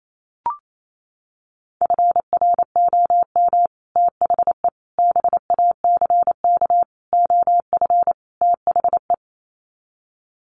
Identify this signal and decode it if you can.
Morse code, heard as a keyed tone, about 28 wpm, 700 Hz. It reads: FROM THE BACK OF THE
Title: Sets of 5 Words Encore 28wpm